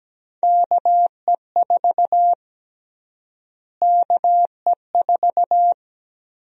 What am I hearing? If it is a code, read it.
Morse code, heard as a keyed tone, about 17 wpm, 700 Hz. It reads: KE4 KE4
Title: USA Call Sign Prefixes - Repeated 2x - 17wpm